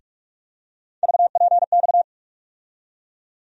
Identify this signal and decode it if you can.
Morse code, heard as a keyed tone, about 46 wpm, 700 Hz. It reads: VPX
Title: USA Call Sign Postfixes - 3 Letters 45wpm